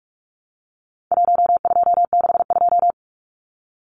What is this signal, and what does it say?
Morse code, heard as a keyed tone, about 44 wpm, 700 Hz. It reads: 1262
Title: Numbers - 4-Digit 45wpm